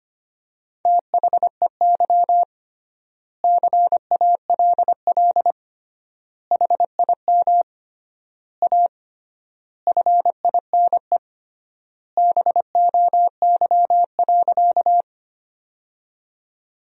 Morse code, heard as a keyed tone, about 25 wpm, 700 Hz. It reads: THEY CALL HIM A FINE BOY.
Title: 3x Word-Spacing - Sentences from Top 500 Words - 25wpm